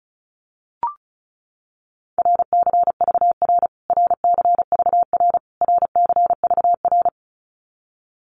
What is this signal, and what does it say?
Morse code, heard as a keyed tone, about 34 wpm, 700 Hz. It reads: RCVR RCVR RCVR
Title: Abbreviations Repeated 3x - 35wpm